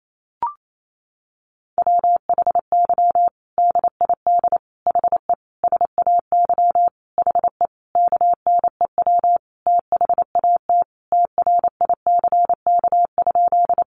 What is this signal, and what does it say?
Morse code, heard as a keyed tone, about 28 wpm, 700 Hz. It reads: WHY DID HE SAY HE KNEW THAT TRICK?
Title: Sentences from Top 1500 Words 28wpm